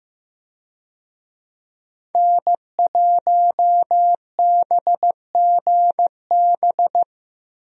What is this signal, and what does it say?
Morse code, heard as a keyed tone, about 15 wpm, 700 Hz. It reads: N1BGB
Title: USA Call Signs - 5 Characters - Repeated 2x - 15wpm